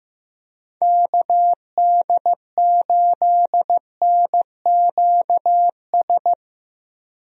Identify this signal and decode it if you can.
Morse code, heard as a keyed tone, about 15 wpm, 700 Hz. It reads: KD8NQS
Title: USA Call Signs - 6 Characters - Repeated 3x - 15wpm